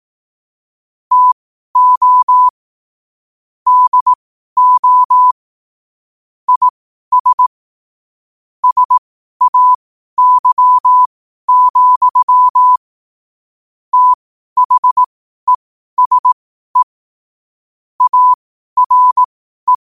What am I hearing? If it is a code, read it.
Morse code, heard as a keyed tone, about 18 wpm, spaced as for 12 wpm, 1000 Hz. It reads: TO DO IS SAY, THESE ARE
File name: qotdfiles / 12wpm